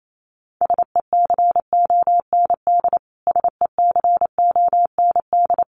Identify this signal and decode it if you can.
Morse code, heard as a keyed tone, about 28 wpm, 700 Hz. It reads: SECOND SECOND
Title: Top 300 Words Repeated 28wpm